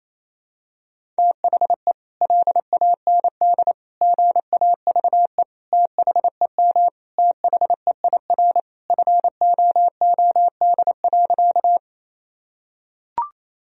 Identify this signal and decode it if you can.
Morse code, heard as a keyed tone, about 28 wpm, 700 Hz. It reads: THE LAND GAVE THEM THEIR FOOD.
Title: Sentences from Top 1000 Words - Review Part 1 - 28wpm